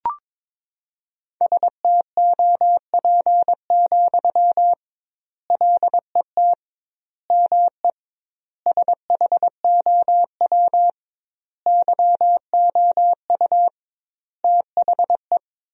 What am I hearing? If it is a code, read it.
Morse code, heard as a keyed tone, about 22 wpm, 700 Hz. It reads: STOP, LET ME SHOW YOU THE
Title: 2x Word-Spacing - Sentences from Top 300 Words - 22wpm